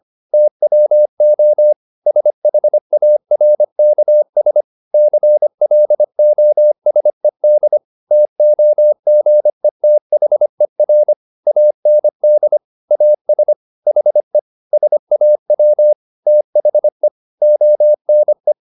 Morse code, heard as a keyed tone, about 25 wpm, 600 Hz. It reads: TWO SHARKS CLOSED TOGETHER AND AS HE SAW THE ONE